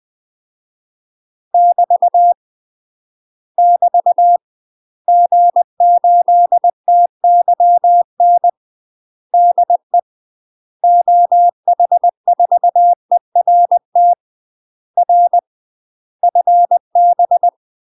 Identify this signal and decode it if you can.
Morse code, heard as a keyed tone, about 20 wpm, 700 Hz. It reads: <BT> <BT> G8TYN DE OH4ERT R FB
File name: UK MASTER QSOs TRAINING / 20WPM / UK MASTER QSOs 0071